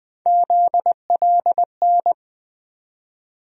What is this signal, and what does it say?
Morse code, heard as a keyed tone, about 20 wpm, 700 Hz. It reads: ZLN
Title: USA Call Sign Postfixes - 3 Letters 20wpm